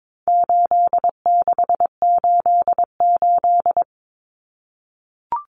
Morse code, heard as a keyed tone, about 22 wpm, 700 Hz. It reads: 8688
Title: Numbers - 4-Digit 22wpm